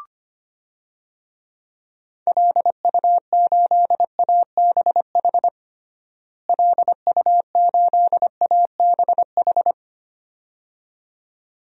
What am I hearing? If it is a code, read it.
Morse code, heard as a keyed tone, about 25 wpm, 700 Hz. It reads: LU8ABH LU8ABH
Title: DXCC-100 Call Signs - 6 Characters - Repeated 2x - 25wpm